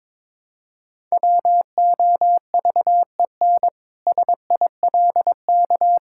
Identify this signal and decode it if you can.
Morse code, heard as a keyed tone, about 22 wpm, 700 Hz. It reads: WOVEN SILK
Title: Pangram Challenge v2 - 22wpm